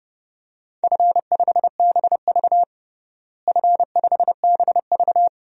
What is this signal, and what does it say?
Morse code, heard as a keyed tone, about 30 wpm, 700 Hz. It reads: F5BV F5BV
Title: DXCC-100 Call Signs - 4 Characters - Repeated 2x - 30wpm